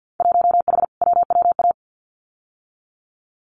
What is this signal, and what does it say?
Morse code, heard as a keyed tone, about 50 wpm, 700 Hz. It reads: 1H WWA
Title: Winter Field Day Exchanges Encore 50wpm